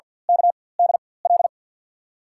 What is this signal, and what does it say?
Morse code, heard as a keyed tone, about 53 wpm, spaced as for 24 wpm, 700 Hz. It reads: KDL
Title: USA Call Signs - 6 Characters - 4x Character Spacing - 50wpm